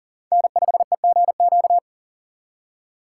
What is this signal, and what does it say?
Morse code, heard as a keyed tone, about 51 wpm, 700 Hz. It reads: N5EGQ